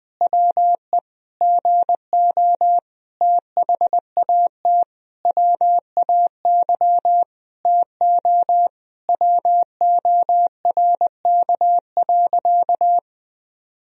Morse code, heard as a keyed tone, about 21 wpm, 700 Hz. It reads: WE GO THAT WAY TO WORK.
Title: Sentences from Top 1000 Words - Review Part 3 - 20wpm